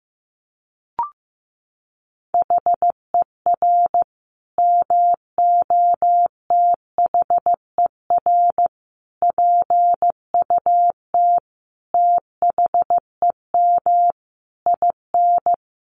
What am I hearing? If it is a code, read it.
Morse code, heard as a keyed tone, about 15 wpm, 700 Hz. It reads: HER MOTHER PUT THEM IN
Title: Sentences from Top 800 Words 15wpm